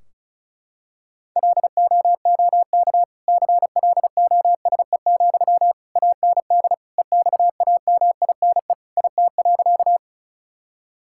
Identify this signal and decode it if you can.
Morse code, heard as a keyed tone, about 36 wpm, 700 Hz. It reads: LOOK CLOSE, AND EXAMINE IT.
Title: Sentences from Top 1600 Words 35wpm